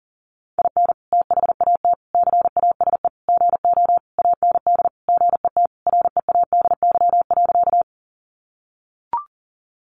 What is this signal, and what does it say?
Morse code, heard as a keyed tone, about 40 wpm, 700 Hz. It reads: IN THAT CASE GO AND GET READY.
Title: Sentences from Top 700 Words 40wpm